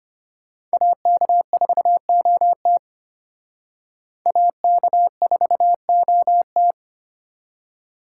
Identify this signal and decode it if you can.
Morse code, heard as a keyed tone, about 31 wpm, 700 Hz. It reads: AK4OT AK4OT